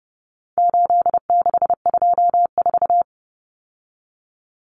Morse code, heard as a keyed tone, about 30 wpm, 700 Hz. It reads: 8624
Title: Common Cut Numbers - 4-Digit 30wpm